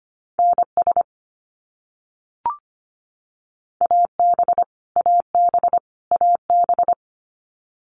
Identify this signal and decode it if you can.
Morse code, heard as a keyed tone, about 25 wpm, 700 Hz. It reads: NS AB AB AB
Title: Canadian Provinces and Territories - Repeated 3x - 25wpm